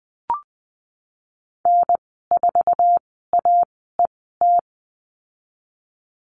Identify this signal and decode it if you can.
Morse code, heard as a keyed tone, about 20 wpm, spaced as for 15 wpm, 700 Hz. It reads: N4AET
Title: 055 Lesson 10g - Letter r - Call Signs 20wpm